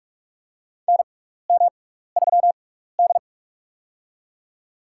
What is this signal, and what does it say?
Morse code, heard as a keyed tone, about 46 wpm, spaced as for 15 wpm, 700 Hz. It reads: NM3D
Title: USA Call Signs - 4 Characters - 6x Character Spacing - 45wpm